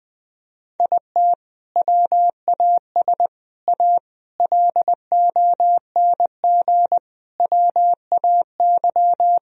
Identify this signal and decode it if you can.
Morse code, heard as a keyed tone, about 21 wpm, 700 Hz. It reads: IT WAS A LONG WAY
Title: Sentences from Top 1000 Words - Review Part 2 - 20wpm